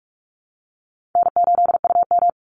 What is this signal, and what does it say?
Morse code, heard as a keyed tone, about 44 wpm, 700 Hz. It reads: N8UM
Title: Easy Call Signs Repeated 3x - 45wpm